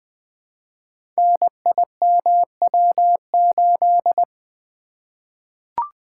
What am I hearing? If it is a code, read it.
Morse code, heard as a keyed tone, about 20 wpm, 700 Hz. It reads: NIMW8
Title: Code Groups - 5 Letter:Numbers - 20wpm